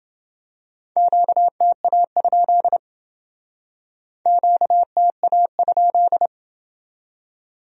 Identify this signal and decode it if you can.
Morse code, heard as a keyed tone, about 29 wpm, 700 Hz. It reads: QTA? QTA?